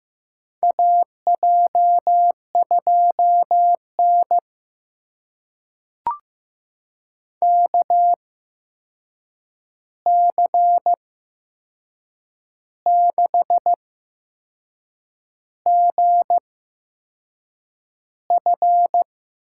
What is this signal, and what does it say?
Morse code, heard as a keyed tone, about 15 wpm, spaced as for 6 wpm, 700 Hz. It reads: AJ2N KC6GF
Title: USA Call Signs - 8x Character Spacing - 15wpm